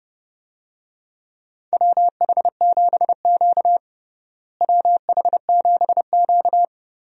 Morse code, heard as a keyed tone, about 30 wpm, 700 Hz. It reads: WH7Q WH7Q
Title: USA Call Signs - 4 Characters - Repeated 3x - 30wpm